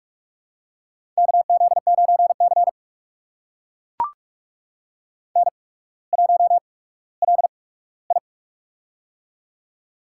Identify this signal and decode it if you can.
Morse code, heard as a keyed tone, about 44 wpm, spaced as for 19 wpm, 700 Hz. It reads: KZ9C N1LI
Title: USA Call Signs - 4 Characters - 8x Character Spacing - 45wpm